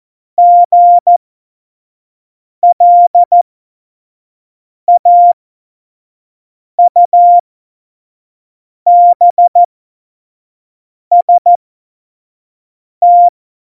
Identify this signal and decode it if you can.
Morse code, heard as a keyed tone, about 14 wpm, spaced as for 5 wpm, 700 Hz. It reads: GLAUBST
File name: Morsekurs / Witze / Witz 14 April 2020, 25 F70